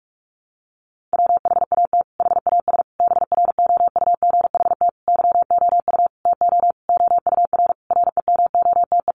Morse code, heard as a keyed tone, about 44 wpm, 700 Hz. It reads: WHAT HAS BROUGHT YOU TO OUR REMOTE
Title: Sentences from Top 1500 Words 45wpm